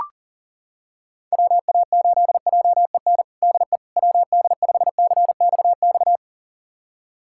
Morse code, heard as a keyed tone, about 40 wpm, 700 Hz. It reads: WA8JEN DE WD5CXX
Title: USA Callsign DE Callsign 40wpm